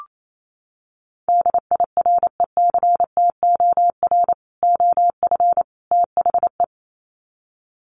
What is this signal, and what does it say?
Morse code, heard as a keyed tone, about 28 wpm, 700 Hz. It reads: DIRECTOR OF THE
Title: Sets of 3 Words 28wpm